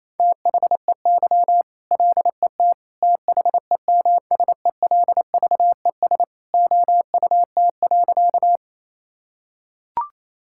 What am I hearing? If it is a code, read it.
Morse code, heard as a keyed tone, about 28 wpm, 700 Hz. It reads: THEY LET THEMSELVES OUT.
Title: Sentences from Top 1000 Words - Review Part 5 - 28wpm